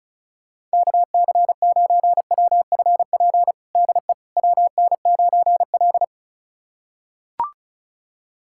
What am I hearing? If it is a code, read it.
Morse code, heard as a keyed tone, about 36 wpm, 700 Hz. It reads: KC9WFP DE WN9L
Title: USA Callsign DE Callsign 35wpm